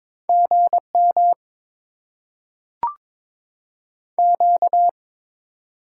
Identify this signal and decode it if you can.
Morse code, heard as a keyed tone, about 22 wpm, 700 Hz. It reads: GM Q
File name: Top 20 QSO Elements - 6x Character Spacing - 22wpm